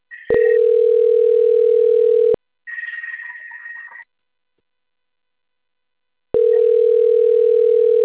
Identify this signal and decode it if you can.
Ringback tone (North America), 440 + 480 Hz.